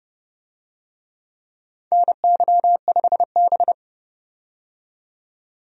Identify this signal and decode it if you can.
Morse code, heard as a keyed tone, about 30 wpm, 700 Hz. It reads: NY5B
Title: USA Call Signs - 4 Characters - 30wpm